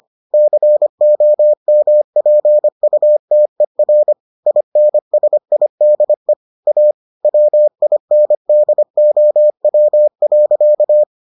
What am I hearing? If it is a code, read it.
Morse code, heard as a keyed tone, about 25 wpm, 600 Hz. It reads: COMPUTER INSIDE A WINDOW.